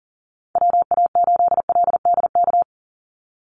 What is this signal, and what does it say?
Morse code, heard as a keyed tone, about 40 wpm, 700 Hz. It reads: WA8LDK
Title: USA Call Signs 40wpm